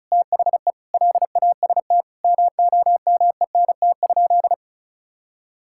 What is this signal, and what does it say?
Morse code, heard as a keyed tone, about 36 wpm, 700 Hz. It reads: THE LAST MOMENT?
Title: Sentences from Top 1600 Words 35wpm